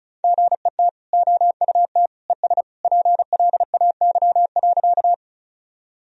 Morse code, heard as a keyed tone, about 34 wpm, 700 Hz. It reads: GET OUT ES PLAY.
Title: Sentences from Top 200 Words and Abbreviations 35wpm